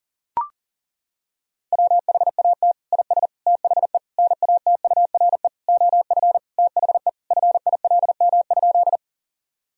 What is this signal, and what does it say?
Morse code, heard as a keyed tone, about 40 wpm, 700 Hz. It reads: WHAT IS THE NATURE OF THE FILM?
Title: Sentences from Top 1000 Words - Review Part 3 - 40wpm